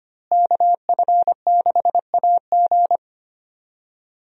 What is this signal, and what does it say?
Morse code, heard as a keyed tone, about 25 wpm, 700 Hz. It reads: KF6AG
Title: USA Call Signs - 5 Characters - Repeated 2x - 25wpm